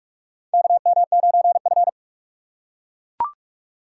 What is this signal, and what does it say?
Morse code, heard as a keyed tone, about 43 wpm, 700 Hz. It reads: KM9F
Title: USA Call Signs - 4 Characters - 6x Character Spacing - 45wpm